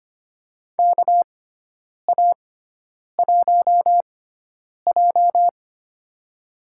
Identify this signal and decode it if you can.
Morse code, heard as a keyed tone, about 25 wpm, spaced as for 9 wpm, 700 Hz. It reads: KA1J
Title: USA Call Signs - 6x Character Spacing - 25wpm